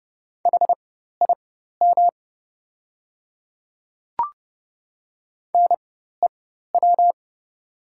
Morse code, heard as a keyed tone, about 30 wpm, spaced as for 14 wpm, 700 Hz. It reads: HIM NEW
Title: 4x Character Spacing - Top 200 Words - 30wpm